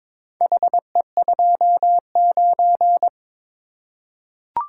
Morse code, heard as a keyed tone, about 22 wpm, 700 Hz. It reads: HE29
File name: practice / 22wpm / Four Letter-Number Combinations - 22wpm